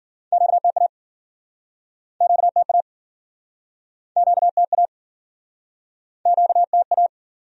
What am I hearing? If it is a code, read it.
Morse code, heard as a keyed tone, about 56 wpm, 700 Hz. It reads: QTA QTA QTA QTA